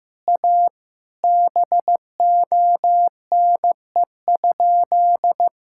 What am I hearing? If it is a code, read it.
Morse code, heard as a keyed tone, about 15 wpm, 700 Hz. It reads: A BONE?
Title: Sentences from Top 1600 Words 15wpm